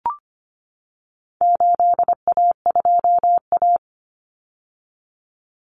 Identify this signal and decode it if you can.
Morse code, heard as a keyed tone, about 25 wpm, 700 Hz. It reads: 8A2A
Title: Common Cut Numbers - 4-Digit 25wpm